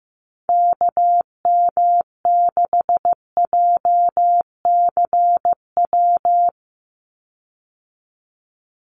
Morse code, heard as a keyed tone, about 15 wpm, 700 Hz. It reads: KM6JCW
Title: USA Call Signs 15wpm